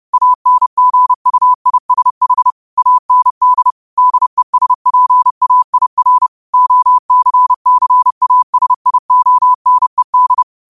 Morse code, heard as a keyed tone, about 30 wpm, 1000 Hz. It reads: ANGUISH AND DESPAIR OCCASIONED
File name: 30wpm